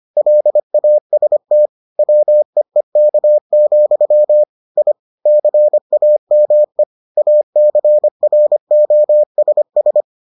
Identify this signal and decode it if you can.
Morse code, heard as a keyed tone, about 25 wpm, 600 Hz. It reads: LAST WEEK, I CAME ACROSS